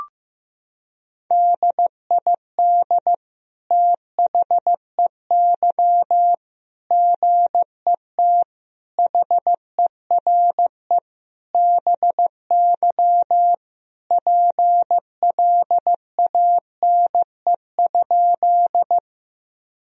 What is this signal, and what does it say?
Morse code, heard as a keyed tone, about 16 wpm, 700 Hz. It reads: DID THEY GET HERE BY PLANE?
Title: Sentences from Top 1000 Words - Review Part 3 - 15wpm